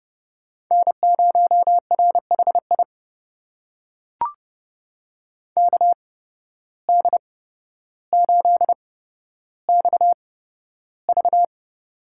Morse code, heard as a keyed tone, about 30 wpm, spaced as for 13 wpm, 700 Hz. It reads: N0RHI KD8XV
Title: USA Call Signs - 5 Characters - 8x Character Spacing - 30wpm